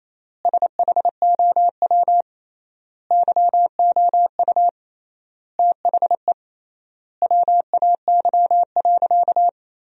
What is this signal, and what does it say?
Morse code, heard as a keyed tone, about 28 wpm, 700 Hz. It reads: SHOW YOU THE WAY.
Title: 3x Word-Spacing - Sentences from Top 300 Words - 28wpm